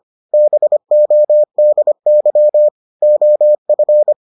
Morse code, heard as a keyed tone, about 25 wpm, 600 Hz. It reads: BODY OF